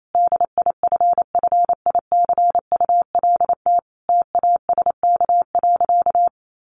Morse code, heard as a keyed tone, about 28 wpm, 700 Hz. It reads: DIFFICULT TASK.
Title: Sentences from Top 1300 Words 28wpm